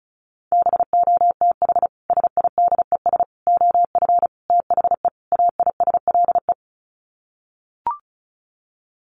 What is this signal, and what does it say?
Morse code, heard as a keyed tone, about 34 wpm, 700 Hz. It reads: BOTH SIDES OF THE AISLE
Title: Sets of 5 Words Encore2 35wpm